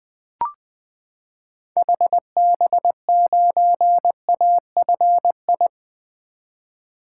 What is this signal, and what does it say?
Morse code, heard as a keyed tone, about 20 wpm, 700 Hz. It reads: HB9AFI
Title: 2023 SOTA Activator Call Signs - 20wpm